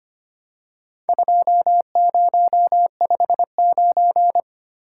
Morse code, heard as a keyed tone, about 25 wpm, 690 Hz. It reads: 2059
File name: practice / 25wpm / Four Letter-Number Combinations - 25wpm